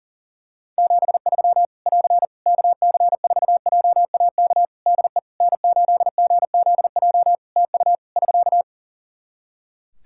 Morse code, heard as a keyed tone, about 40 wpm, 700 Hz. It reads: 73 <AR> KC4JAK DE N8GZJ TU <SK>
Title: Practice QSO - Exchange 4 - Ending - 40wpm